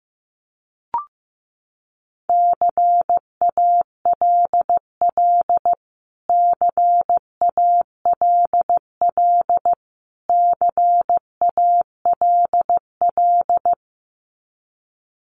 Morse code, heard as a keyed tone, about 15 wpm, 700 Hz. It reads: CALL CALL CALL
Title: Top 500 Words Repeated 15wpm